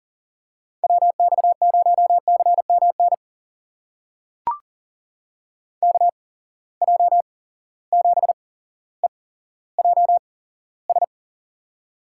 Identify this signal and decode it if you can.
Morse code, heard as a keyed tone, about 40 wpm, spaced as for 17 wpm, 700 Hz. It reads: WX0CMN KJ7EJS
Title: USA Call Signs - 6 Characters - 8x Character Spacing - 40wpm